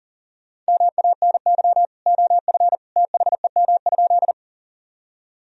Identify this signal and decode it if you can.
Morse code, heard as a keyed tone, about 40 wpm, 700 Hz. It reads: MANY OF THEM?
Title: Sentences from Top 1000 Words - Review Part 1 - 40wpm